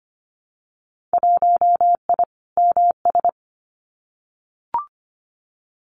Morse code, heard as a keyed tone, about 25 wpm, 700 Hz. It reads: 1I MS
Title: Winter Field Day Exchanges Encore 25wpm